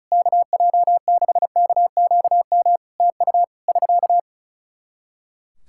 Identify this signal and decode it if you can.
Morse code, heard as a keyed tone, about 34 wpm, 700 Hz. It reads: KJ6KQM TU <SK>
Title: Practice QSO - Exchange 4 - Ending - 35wpm